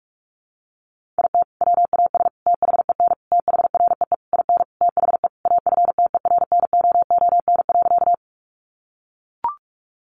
Morse code, heard as a keyed tone, about 44 wpm, 700 Hz. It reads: IT WAS THEN THREE IN THE AFTERNOON.
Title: Sentences from Top 1500 Words 45wpm